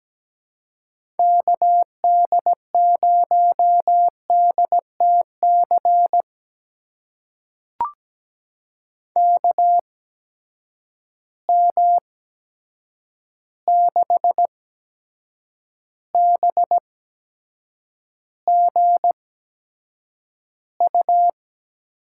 Morse code, heard as a keyed tone, about 17 wpm, spaced as for 7 wpm, 700 Hz. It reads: KD0DTC KM6BGU
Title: USA Call Signs - 6 Characters - 8x Character Spacing - 17wpm